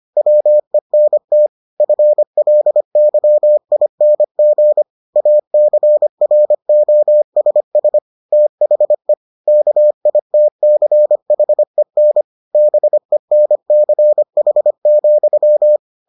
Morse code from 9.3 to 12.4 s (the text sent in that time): KITCHEN